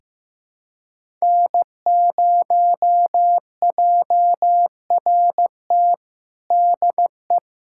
Morse code, heard as a keyed tone, about 16 wpm, 700 Hz. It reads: N0JRT DE
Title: USA Callsign DE Callsign - 5 Characters - 15wpm